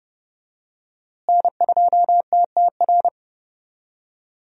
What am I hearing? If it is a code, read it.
Morse code, heard as a keyed tone, about 30 wpm, 700 Hz. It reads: N2TTR
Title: USA Call Signs - 5 Characters - Repeated 2x - 30wpm